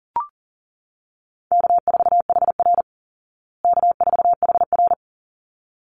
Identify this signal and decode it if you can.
Morse code, heard as a keyed tone, about 40 wpm, 700 Hz. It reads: K4HR K4HR
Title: Easy Call Signs Repeated 2x - 40wpm